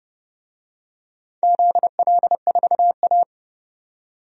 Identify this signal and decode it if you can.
Morse code, heard as a keyed tone, about 32 wpm, 700 Hz. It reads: ZL4A
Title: DXCC-100 Call Signs - 4 Characters - 30wpm